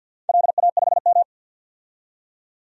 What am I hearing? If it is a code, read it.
Morse code, heard as a keyed tone, about 51 wpm, 700 Hz. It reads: LA5M
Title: DXCC-100 Call Signs - 4 Characters - 50wpm